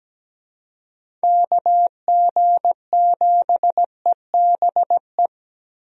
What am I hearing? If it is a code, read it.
Morse code, heard as a keyed tone, about 17 wpm, 700 Hz. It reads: KG7EBE
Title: USA Call Signs - 6 Characters - Repeated 3x - 17wpm